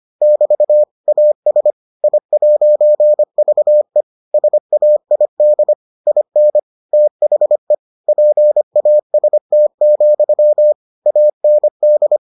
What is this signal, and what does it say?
Morse code, heard as a keyed tone, about 25 wpm, 600 Hz. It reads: <BT> AS I'VE SAID IN THE PAST, AND